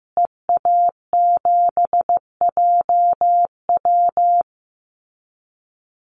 Morse code, heard as a keyed tone, about 15 wpm, 700 Hz. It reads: EA7JW
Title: Call Signs Repeated 2x - 15wpm